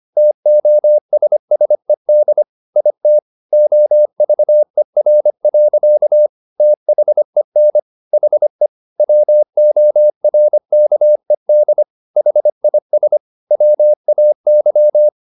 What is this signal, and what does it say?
Morse code, heard as a keyed tone, about 25 wpm, 600 Hz. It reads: TOSSED IT OVER. THEN HE WORKED HIS WAY